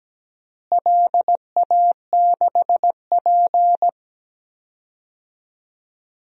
Morse code, heard as a keyed tone, about 17 wpm, 700 Hz. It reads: LA6P
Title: DXCC-100 Call Signs - 4 Characters - Repeated 2x - 17wpm